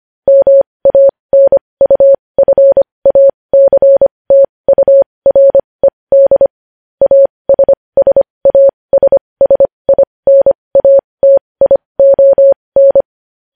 Morse code, heard as a keyed tone, about 25 wpm, spaced as for 20 wpm, 550 Hz. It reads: MANUFACTURED ASSASSINATION
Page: News Headlines in Morse Code at 20 WPM